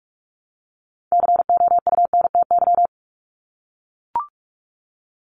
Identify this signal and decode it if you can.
Morse code, heard as a keyed tone, about 44 wpm, 700 Hz. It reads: COUNTY
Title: Top 1001-1250 Words 45wpm